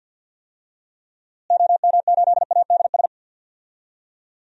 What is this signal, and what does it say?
Morse code, heard as a keyed tone, about 52 wpm, 700 Hz. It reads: OM8ADS
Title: DXCC-100 Call Signs - 6 Characters - Repeated 3x - 50wpm